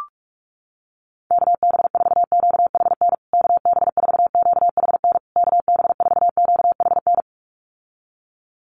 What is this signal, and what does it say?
Morse code, heard as a keyed tone, about 44 wpm, 700 Hz. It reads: KB4QHN KB4QHN KB4QHN
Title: USA Call Signs Repeated 3x 45wpm